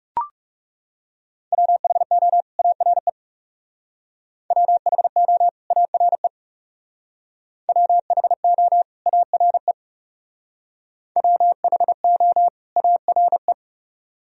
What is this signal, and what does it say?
Morse code, heard as a keyed tone, about 39 wpm, 700 Hz. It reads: WHO ARE WHO ARE WHO ARE WHO ARE